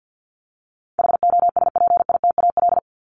Morse code, heard as a keyed tone, about 50 wpm, 700 Hz. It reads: HOSPITAL
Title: Top 1751-2000 Words 50wpm